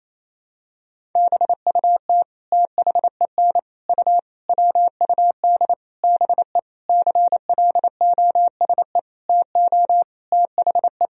Morse code, heard as a keyed tone, about 28 wpm, 700 Hz. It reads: BUT THEN U WUD BE CLOSE TO THE
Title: Sentences from Top 300 Words and Abbreviations 28wpm